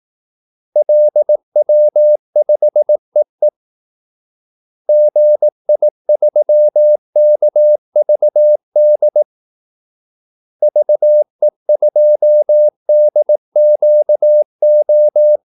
Morse code, heard as a keyed tone, about 18 wpm, 600 Hz. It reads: LW5EE GI3KVD VE2DQO
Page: Kaldesignaler 18 wpm